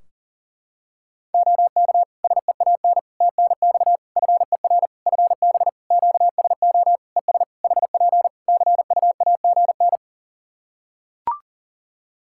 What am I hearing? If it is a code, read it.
Morse code, heard as a keyed tone, about 40 wpm, 700 Hz. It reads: OK SEAN TNX FER FB QSO ES HP CUAGN
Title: Practice QSO - Exchange 4 - Ending - 40wpm